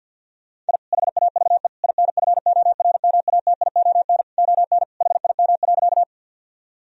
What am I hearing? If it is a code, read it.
Morse code, heard as a keyed tone, about 50 wpm, 700 Hz. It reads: I HAVE INFORMATION ON HIM.
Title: Sentences from Top 1000 Words - Review Part 3 - 50wpm